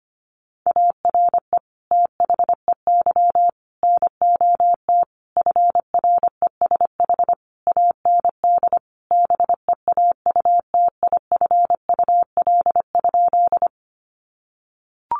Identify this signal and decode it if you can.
Morse code, heard as a keyed tone, about 25 wpm, 700 Hz. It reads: ARE THEY NOT FRESH AND BEAUTIFUL?
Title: Sentences from Top 1200 Words 25wpm